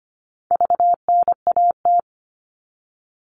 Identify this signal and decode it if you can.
Morse code, heard as a keyed tone, about 25 wpm, 700 Hz. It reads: VNAT
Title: All Cut Numbers - 4-Digit 25wpm